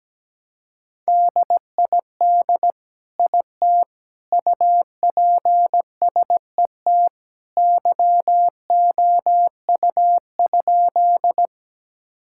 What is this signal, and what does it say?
Morse code, heard as a keyed tone, about 17 wpm, 700 Hz. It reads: DID IT UPSET YOU?
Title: Sentences from Top 1000 Words - Review Part 5 - 17wpm